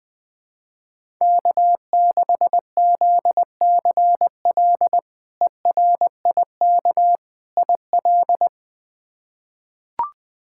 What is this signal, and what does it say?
Morse code, heard as a keyed tone, about 20 wpm, 700 Hz. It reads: K6ZCL ERIK IL
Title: K1USN SST QSO Practice - 20wpm